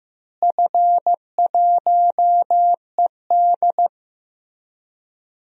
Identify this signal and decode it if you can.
Morse code, heard as a keyed tone, about 15 wpm, 700 Hz. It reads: F1ED
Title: DXCC-100 Call Signs - 4 Characters - Repeated 3x - 15wpm